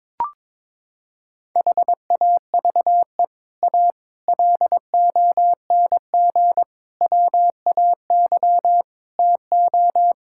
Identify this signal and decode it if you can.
Morse code, heard as a keyed tone, about 22 wpm, 700 Hz. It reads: HAVE A LONG WAY TO
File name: practice / 22wpm / Sets of 6 Words - 22wpm